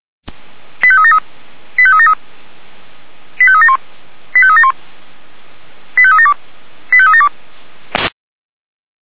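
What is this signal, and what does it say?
Five-tone selective call (ZVEI), heard as quick runs of five tones, tones of about 70 ms. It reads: 85373 85373 85371 85371 85372 85372
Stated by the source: ZVEI five-tone selective call 85392 received on SDR.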